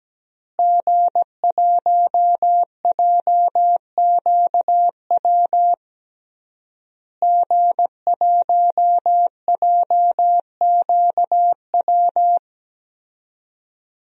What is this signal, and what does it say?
Morse code, heard as a keyed tone, about 17 wpm, 700 Hz. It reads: G1JQW G1JQW
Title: DXCC-100 Call Signs - 5 Characters - Repeated 3x - 17wpm